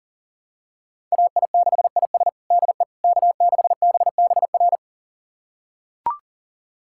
Morse code, heard as a keyed tone, about 40 wpm, 700 Hz. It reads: AI6IS DE K6BBR
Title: USA Callsign DE Callsign - 5 Characters - 40wpm